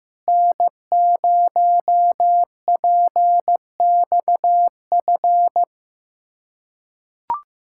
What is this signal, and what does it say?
Morse code, heard as a keyed tone, about 15 wpm, 700 Hz. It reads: N0PXF
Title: USA Callsign DE Callsign - 5 Characters - 15wpm